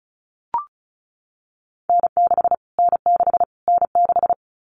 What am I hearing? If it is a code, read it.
Morse code, heard as a keyed tone, about 34 wpm, 700 Hz. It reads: N6 N6 N6
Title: Callsign Prefixes for Top 25 DXCC - Repeated 3x - 35wpm